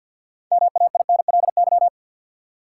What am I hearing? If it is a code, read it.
Morse code, heard as a keyed tone, about 53 wpm, 700 Hz. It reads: MAINLY
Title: Six Letter Words - 50wpm